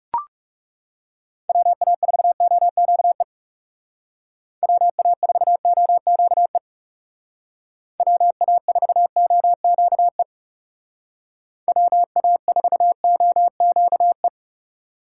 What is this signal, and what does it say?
Morse code, heard as a keyed tone, about 39 wpm, 700 Hz. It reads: WA4OQE WA4OQE WA4OQE WA4OQE